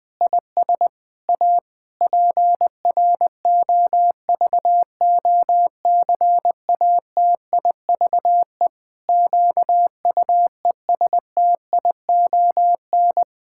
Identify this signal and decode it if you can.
Morse code, heard as a keyed tone, about 21 wpm, 700 Hz. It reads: IS A PROVOCATIVE QUESTION
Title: Pangram Challenge v2 - 20wpm